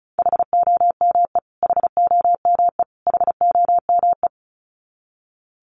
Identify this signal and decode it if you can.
Morse code, heard as a keyed tone, about 34 wpm, 700 Hz. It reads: HOME HOME HOME
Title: Top 200 Words Repeated 35wpm